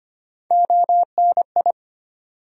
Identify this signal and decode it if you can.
Morse code, heard as a keyed tone, about 25 wpm, 700 Hz. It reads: ONI
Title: USA Call Sign Postfixes - 3 Letters 25wpm